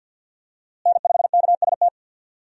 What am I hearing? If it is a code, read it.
Morse code, heard as a keyed tone, about 50 wpm, 700 Hz. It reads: N5KST
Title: USA Call Signs - 5 Characters - Repeated 2x - 50wpm